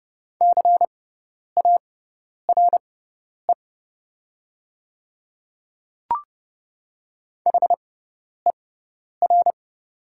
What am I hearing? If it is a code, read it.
Morse code, heard as a keyed tone, about 30 wpm, spaced as for 10 wpm, 700 Hz. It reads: CARE HER
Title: 6x Character Spacing - Top 300 Words - 30wpm